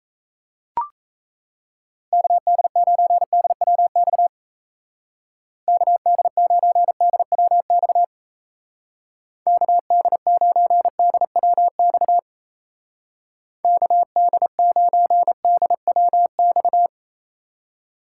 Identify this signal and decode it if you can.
Morse code, heard as a keyed tone, about 36 wpm, 700 Hz. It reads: KD9DWX KD9DWX KD9DWX KD9DWX